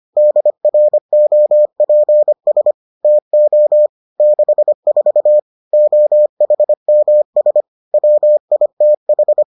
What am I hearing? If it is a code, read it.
Morse code, heard as a keyed tone, about 25 wpm, 600 Hz. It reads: DROPS TO 64 OHMS WITH